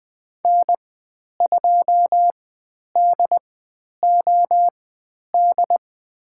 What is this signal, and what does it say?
Morse code, heard as a keyed tone, about 20 wpm, spaced as for 10 wpm, 700 Hz. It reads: N2DOD
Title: 102 Lesson 17g - Letter u - Call Signs 20wpm